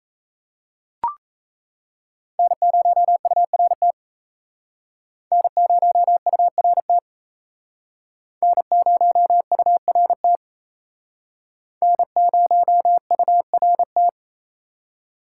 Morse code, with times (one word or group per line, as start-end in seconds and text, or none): 2.4-3.9 N0URT
5.3-7.0 N0URT
8.4-10.4 N0URT
11.8-14.1 N0URT